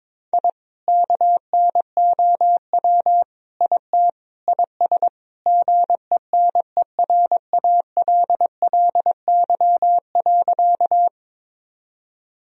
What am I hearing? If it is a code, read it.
Morse code, heard as a keyed tone, about 22 wpm, 700 Hz. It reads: I KNOW IT IS GENERALLY.
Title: Sentences from Top 1000 Words - Review Part 4 - 22wpm